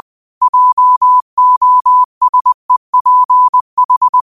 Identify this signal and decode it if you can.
Morse code, heard as a keyed tone, about 20 wpm, 1000 Hz. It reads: JOSEPH